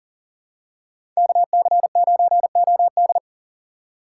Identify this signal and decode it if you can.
Morse code, heard as a keyed tone, about 37 wpm, 700 Hz. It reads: KC9OD
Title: USA Call Signs - 5 Characters - 3x Character Spacing - 40wpm